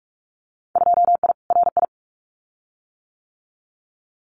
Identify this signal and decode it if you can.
Morse code, heard as a keyed tone, about 44 wpm, 700 Hz. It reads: 2I RI
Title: Winter Field Day Exchanges Encore 45wpm